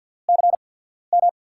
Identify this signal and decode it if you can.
Morse code, heard as a keyed tone, about 53 wpm, spaced as for 14 wpm, 700 Hz. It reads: CM
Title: USA Call Signs - 6 Characters - 8x Character Spacing - 50wpm